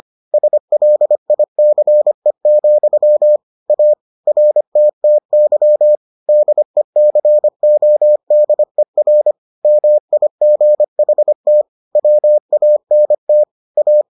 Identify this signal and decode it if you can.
Morse code, heard as a keyed tone, about 25 wpm, 600 Hz. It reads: SLICE, A RTTY DECODER MIGHT WANT A